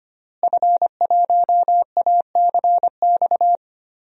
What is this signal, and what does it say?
Morse code, heard as a keyed tone, about 25 wpm, 700 Hz. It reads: F1ACX
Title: DXCC-100 Call Signs - 5 Characters - Repeated 3x - 25wpm